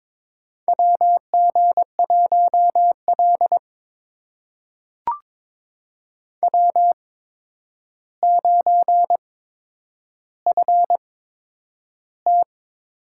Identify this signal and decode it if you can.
Morse code, heard as a keyed tone, about 22 wpm, spaced as for 9 wpm, 700 Hz. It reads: WG1L W9FT
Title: USA Call Signs - 4 Characters - 8x Character Spacing - 22wpm